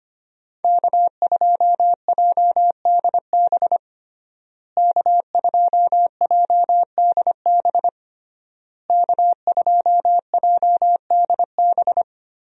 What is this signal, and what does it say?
Morse code, heard as a keyed tone, about 25 wpm, 700 Hz. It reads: K2JDB K2JDB K2JDB
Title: USA Call Signs - 5 Characters - Repeated 3x - 25wpm